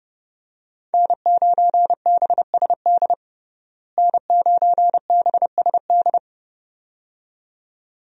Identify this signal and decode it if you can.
Morse code, heard as a keyed tone, about 30 wpm, 700 Hz. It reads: N9BSD N9BSD
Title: USA Call Signs - 5 Characters - Repeated 2x - 30wpm